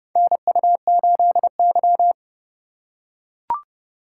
Morse code, heard as a keyed tone, about 32 wpm, 700 Hz. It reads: NU8Y
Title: USA Call Signs - 4 Characters - 4x Character Spacing - 30wpm